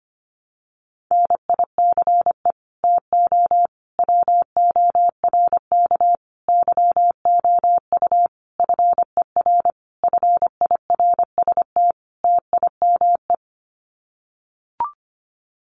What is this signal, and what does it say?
Morse code, heard as a keyed tone, about 25 wpm, 700 Hz. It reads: NICE TO WORK YOU FER FIRST TIME
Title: Common QSO Phrases 25wpm